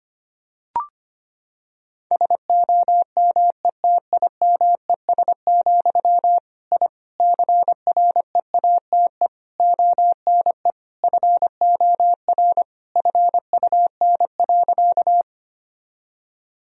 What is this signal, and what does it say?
Morse code, heard as a keyed tone, about 25 wpm, 700 Hz. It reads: SOMETIMES, I CREATE ONE FOR FUN.
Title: Sentences from Top 1000 Words - Review Part 3 - 25wpm